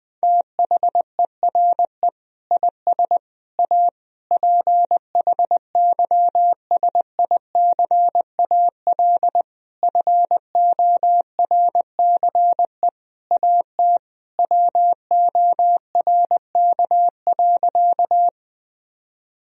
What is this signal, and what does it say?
Morse code, heard as a keyed tone, about 20 wpm, 700 Hz. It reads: THERE IS A PHYSICAL FORCE AT WORK.
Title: Sentences from Top 1000 Words - Review Part 4 - 20wpm